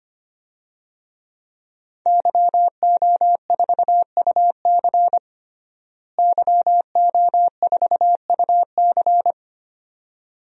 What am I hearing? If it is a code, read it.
Morse code, heard as a keyed tone, about 25 wpm, 700 Hz. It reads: YO4UC YO4UC
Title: DXCC-100 Call Signs - 5 Characters - Repeated 2x - 25wpm